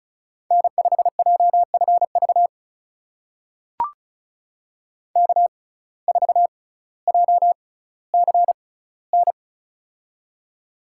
Morse code, heard as a keyed tone, about 34 wpm, spaced as for 18 wpm, 700 Hz. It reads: N5JFV K4JCN
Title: USA Call Signs - 5 Characters - 6x Character Spacing - 35wpm